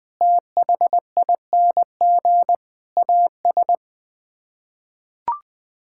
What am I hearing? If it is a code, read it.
Morse code, heard as a keyed tone, about 20 wpm, 700 Hz. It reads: THING AS
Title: Sets of 6 Words - 20wpm